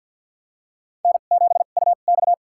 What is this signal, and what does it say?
Morse code, heard as a keyed tone, about 52 wpm, spaced as for 36 wpm, 690 Hz. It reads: N7UX
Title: USA Call Signs - 2x Character Spacing - 50wpm